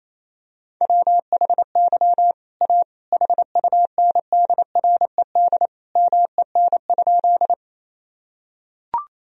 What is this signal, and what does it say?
Morse code, heard as a keyed tone, about 28 wpm, 700 Hz. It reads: WHY A HUNDRED MEN?
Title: Sentences from Top 1000 Words - Review Part 2 - 28wpm